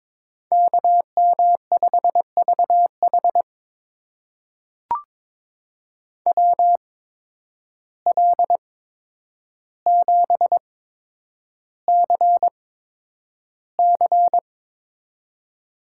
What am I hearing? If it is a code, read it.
Morse code, heard as a keyed tone, about 22 wpm, spaced as for 9 wpm, 700 Hz. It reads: KM5VH WL7CC
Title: USA Call Signs - 5 Characters - 8x Character Spacing - 22wpm